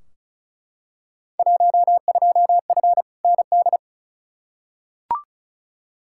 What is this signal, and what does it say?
Morse code, heard as a keyed tone, about 36 wpm, 700 Hz. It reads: 12F ND
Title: Field Day Exchanges Encore - 35wpm